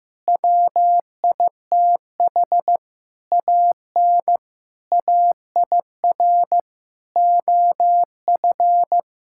Morse code, heard as a keyed tone, about 15 wpm, 700 Hz. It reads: WITH AN AIR OF
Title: Sentences from Top 1600 Words 15wpm